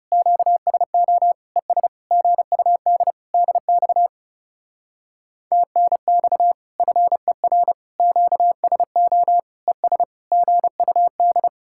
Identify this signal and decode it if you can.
Morse code, heard as a keyed tone, about 39 wpm, 700 Hz. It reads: QSO ES GUD DX TNX FER QSO ES GUD